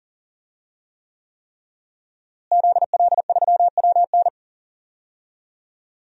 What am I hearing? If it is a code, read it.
Morse code, heard as a keyed tone, about 40 wpm, 700 Hz. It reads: ZL3WN
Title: DXCC-100 Call Signs - 5 Characters - Repeated 3x - 40wpm